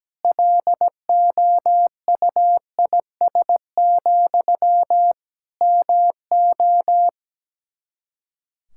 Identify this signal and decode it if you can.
Morse code, heard as a keyed tone, about 17 wpm, 700 Hz. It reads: LOUIS, MO
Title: Practice QSO - Exchange 1 - Beginning - 17wpm